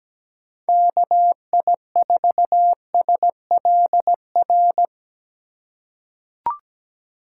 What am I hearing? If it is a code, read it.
Morse code, heard as a keyed tone, about 17 wpm, 700 Hz. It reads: KI4SLR
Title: USA Call Signs - 6 Characters - 2x Character Spacing - 17wpm